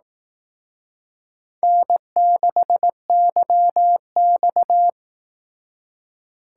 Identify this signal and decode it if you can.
Morse code, heard as a keyed tone, about 18 wpm, 700 Hz. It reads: N6YX
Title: Speed Racing - USA Call Signs 15wpm